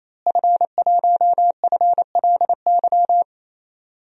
Morse code, heard as a keyed tone, about 28 wpm, 700 Hz. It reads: F1FLY
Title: DXCC-100 Call Signs - 5 Characters - Repeated 3x - 28wpm